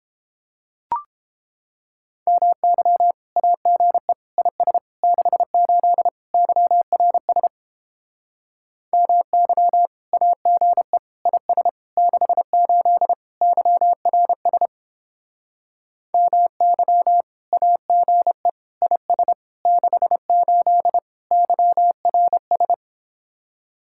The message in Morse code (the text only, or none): MY AGE IS 68 YRS MY AGE IS 68 YRS MY AGE IS 68 YRS